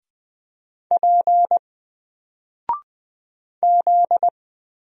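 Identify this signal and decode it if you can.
Morse code, heard as a keyed tone, about 20 wpm, 700 Hz. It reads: P Z
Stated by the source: Single letter and number Morse drill at 20 wpm, from N4LCD.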